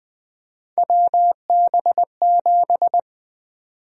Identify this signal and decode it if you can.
Morse code, heard as a keyed tone, about 21 wpm, 700 Hz. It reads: WB7
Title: USA Call Sign Prefixes - 20wpm